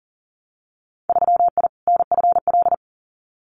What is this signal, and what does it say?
Morse code, heard as a keyed tone, about 39 wpm, 700 Hz. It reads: 3I NFL
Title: Winter Field Day Exchanges 40wpm